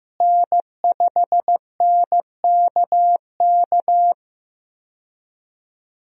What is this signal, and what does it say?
Morse code, heard as a keyed tone, about 15 wpm, 700 Hz. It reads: N5NKK
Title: USA Call Signs - 5 Characters - Repeated 2x - 15wpm